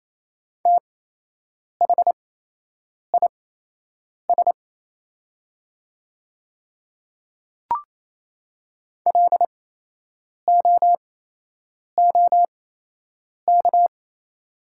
Morse code, heard as a keyed tone, about 28 wpm, spaced as for 8 wpm, 700 Hz. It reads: THIS LOOK
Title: 8x Character Spacing - Top 100 Words - 28wpm